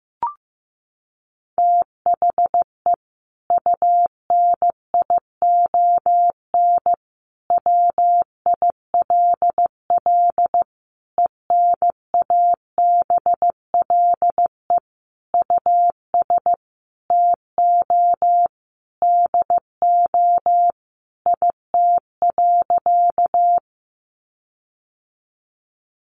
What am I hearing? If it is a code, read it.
Morse code, heard as a keyed tone, about 15 wpm, 700 Hz. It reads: THE UNION WILL ENABLE US TO DO IT.
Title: Sentences from Top 1500 Words 15wpm